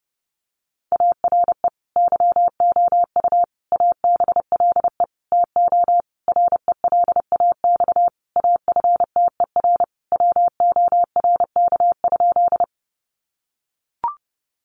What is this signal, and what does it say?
Morse code, heard as a keyed tone, about 30 wpm, 700 Hz. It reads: ARE YOU ABLE TO RELAX AFTER WORK?
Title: Sentences from Top 1400 Words 30wpm